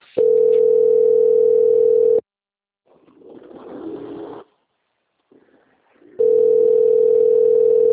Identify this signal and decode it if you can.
Ringback tone (North America), 440 + 480 Hz.